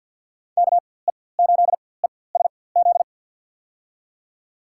Morse code, heard as a keyed tone, about 52 wpm, spaced as for 24 wpm, 700 Hz. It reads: KE8ESZ
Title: USA Call Signs - 4x Character Spacing - 50wpm